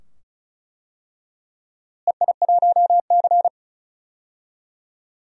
Morse code, heard as a keyed tone, about 34 wpm, 700 Hz. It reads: EI1C
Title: DXCC-100 Call Signs - 4 Characters - Repeated 2x - 35wpm